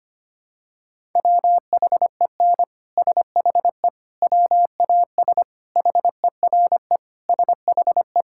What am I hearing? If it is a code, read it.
Morse code, heard as a keyed tone, about 25 wpm, 700 Hz. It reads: WHEN SHE WAS HERE SHE
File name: practice / 25wpm / Sentences with 3-Word Phrases - 25wpm